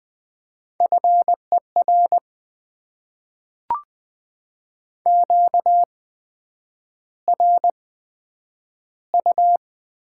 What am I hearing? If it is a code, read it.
Morse code, heard as a keyed tone, about 20 wpm, spaced as for 9 wpm, 700 Hz. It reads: FER QRU
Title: Top 90 QSO Elements - 8x Character Spacing - 20wpm